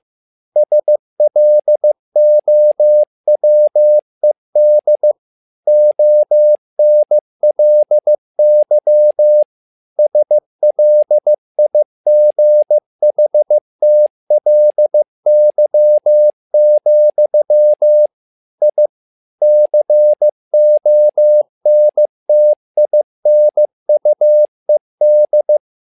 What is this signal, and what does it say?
Morse code, heard as a keyed tone, about 15 wpm, 600 Hz. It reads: SLOWED ONLY SLIGHTLY, I CONTINUED
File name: learning center / tower trouble / tower trouble 15wpm